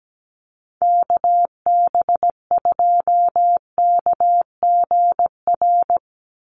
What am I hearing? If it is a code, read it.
Morse code, heard as a keyed tone, about 17 wpm, 700 Hz. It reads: KB2KGR
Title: USA Call Signs 17wpm